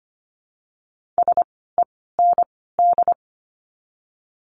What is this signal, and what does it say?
Morse code, heard as a keyed tone, about 24 wpm, spaced as for 16 wpm, 700 Hz. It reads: SEND
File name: course / 25:10wpm / 094 Lesson 16f - Letter c - Words 25wpm